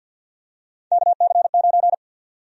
Morse code, heard as a keyed tone, about 53 wpm, 700 Hz. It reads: KC9
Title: USA Call Sign Prefixes - 50wpm